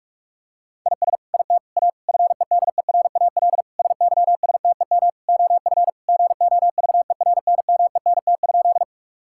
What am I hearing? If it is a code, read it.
Morse code, heard as a keyed tone, about 41 wpm, 700 Hz. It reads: IS IT A FEDERAL SYSTEM OF GOVERNMENT?
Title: Sentences from Top 1000 Words - Review Part 5 - 45wpm